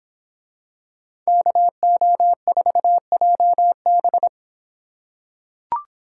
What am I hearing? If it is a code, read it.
Morse code, heard as a keyed tone, about 23 wpm, 700 Hz. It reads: KO4JB